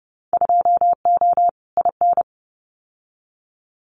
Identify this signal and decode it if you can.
Morse code, heard as a keyed tone, about 30 wpm, 700 Hz. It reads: 2O IN
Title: Winter Field Day Exchanges 30wpm